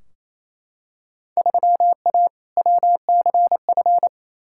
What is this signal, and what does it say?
Morse code, heard as a keyed tone, about 28 wpm, 700 Hz. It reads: 3A WCF
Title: Field Day Exchanges Encore - 28wpm